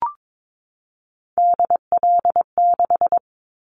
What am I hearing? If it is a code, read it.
Morse code, heard as a keyed tone, about 22 wpm, 700 Hz. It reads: DL6
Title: Callsign Prefixes for Top 25 DXCC - 22wpm